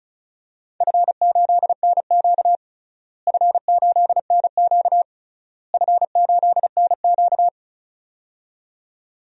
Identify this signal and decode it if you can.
Morse code, heard as a keyed tone, about 36 wpm, 700 Hz. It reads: F8NQ F8NQ F8NQ
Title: DXCC-100 Call Signs - 4 Characters - Repeated 3x - 35wpm